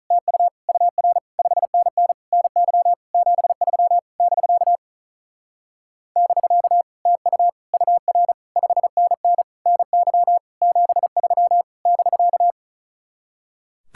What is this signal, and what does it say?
Morse code, heard as a keyed tone, about 46 wpm, 700 Hz. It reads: TU UR 5NN NY 73 <BK> <BK> TU UR 5NN NY 73 <BK>